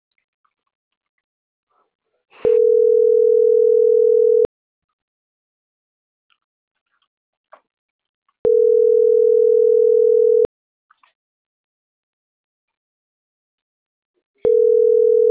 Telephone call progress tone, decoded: ringback tone

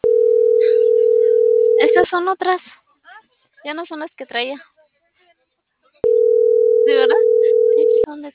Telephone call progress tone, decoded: ringback tone

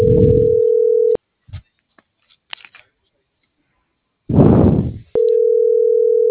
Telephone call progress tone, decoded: ringback tone

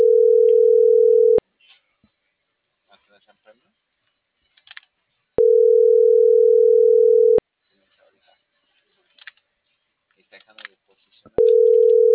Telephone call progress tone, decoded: ringback tone

